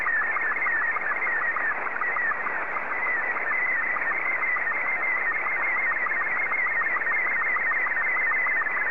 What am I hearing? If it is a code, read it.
SSTV (slow-scan television), Robot 36.
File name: sstv / R36-JA3QJQ